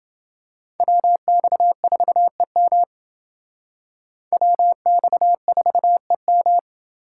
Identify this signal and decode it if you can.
Morse code, heard as a keyed tone, about 28 wpm, 700 Hz. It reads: WX4EM WX4EM